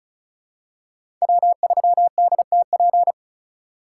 Morse code, heard as a keyed tone, about 36 wpm, 700 Hz. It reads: W3DTP
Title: USA Call Signs - 5 Characters - 2x Character Spacing - 35wpm